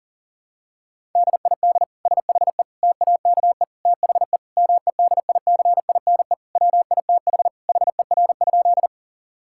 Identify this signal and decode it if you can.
Morse code, heard as a keyed tone, about 40 wpm, 700 Hz. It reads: DID SHE TAKE THE MEDICINE WITH HER?
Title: Sentences from Top 1000 Words - Review Part 6 - 40wpm